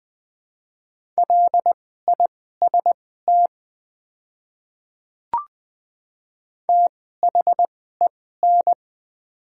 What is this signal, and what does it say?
Morse code, heard as a keyed tone, about 20 wpm, spaced as for 14 wpm, 700 Hz. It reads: LIST THEN
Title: 2x Character Spacing - Top 500 Words - 20wpm